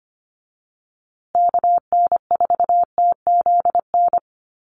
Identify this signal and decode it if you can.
Morse code, heard as a keyed tone, about 25 wpm, 700 Hz. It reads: KN4TZN
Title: USA Call Signs Repeated 2x 25wpm